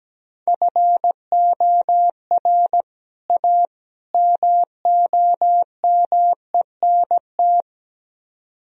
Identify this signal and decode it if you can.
Morse code, heard as a keyed tone, about 18 wpm, 700 Hz. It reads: FOR A MOMENT
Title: Speed-Racing - Sets of 3 Words 17wpm